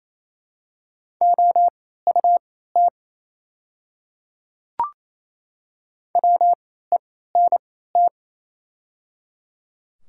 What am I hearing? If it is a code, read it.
Morse code, heard as a keyed tone, about 27 wpm, spaced as for 16 wpm, 700 Hz. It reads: OUT WENT
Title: 3x Character Spacing - Top 200 Words - 28wpm